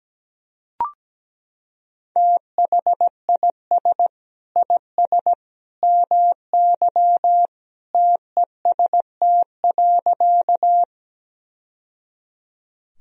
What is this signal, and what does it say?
Morse code, heard as a keyed tone, about 17 wpm, 700 Hz. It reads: THIS IS MY TEST.
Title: Sentences from Top 1000 Words - Review Part 3 - 17wpm